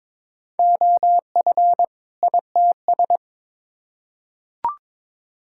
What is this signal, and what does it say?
Morse code, heard as a keyed tone, about 22 wpm, 700 Hz. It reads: OF ITS
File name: Build A Phrase - 2-Word Phrases - 22wpm